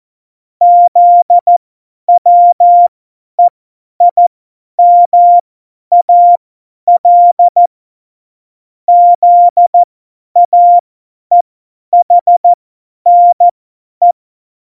Morse code, heard as a keyed tone, about 14 wpm, spaced as for 10 wpm, 700 Hz. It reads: ZWEIMAL ZAEHNE
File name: Morsekurs / Witze / Witz 26. September 17, Tempo 50